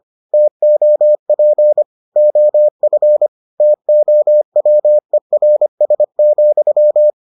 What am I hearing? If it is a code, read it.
Morse code, heard as a keyed tone, about 25 wpm, 600 Hz. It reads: TOP OF TOWERS,